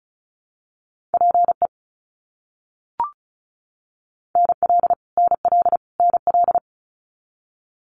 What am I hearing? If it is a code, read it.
Morse code, heard as a keyed tone, about 36 wpm, 700 Hz. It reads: PE NL NL NL
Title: Canadian Provinces and Territories - Repeated 3x - 35wpm